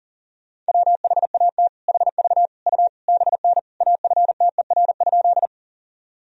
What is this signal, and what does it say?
Morse code, heard as a keyed tone, about 40 wpm, 700 Hz. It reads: WHAT HV U BN AFTER?
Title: Sentences from Top 200 Words and Abbreviations 40wpm